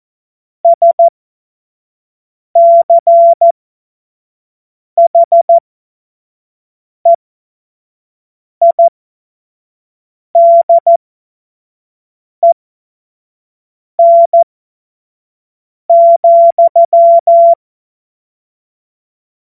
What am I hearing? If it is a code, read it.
Morse code, heard as a keyed tone, about 14 wpm, spaced as for 5 wpm, 680 Hz. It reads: SCHEIDEN,
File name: Morsekurs / Witze / Witz 22. Februar 2022 25 F70